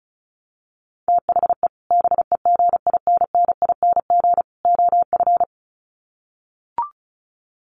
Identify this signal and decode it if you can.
Morse code, heard as a keyed tone, about 34 wpm, 700 Hz. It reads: THE BEGINNING OF
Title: Sets of 3 Words 35wpm